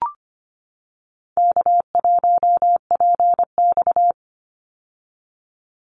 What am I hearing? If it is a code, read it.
Morse code, heard as a keyed tone, about 25 wpm, 700 Hz. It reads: K1PX
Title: Easy Call Signs - 25wpm